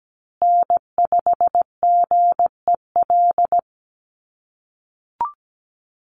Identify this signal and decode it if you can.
Morse code, heard as a keyed tone, about 17 wpm, 700 Hz. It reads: N5GEL
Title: USA Call Signs Repeated 2x 17wpm